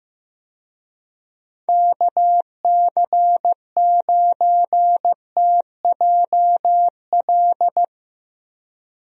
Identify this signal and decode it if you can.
Morse code, heard as a keyed tone, about 15 wpm, 700 Hz. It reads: KC9TJL
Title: USA Call Signs - 6 Characters - Repeated 3x - 15wpm